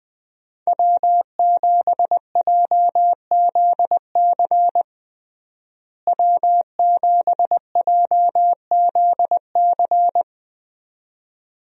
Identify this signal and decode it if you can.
Morse code, heard as a keyed tone, about 20 wpm, 700 Hz. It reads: W7JZC W7JZC
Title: USA Call Signs - 5 Characters - Repeated 2x - 20wpm